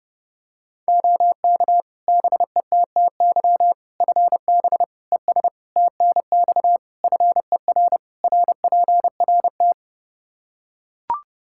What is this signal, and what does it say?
Morse code, heard as a keyed tone, about 32 wpm, 700 Hz. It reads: OK BETTY FB ES TNX FER RPRT
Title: Practice QSO - Exchange 2 - Info - 30wpm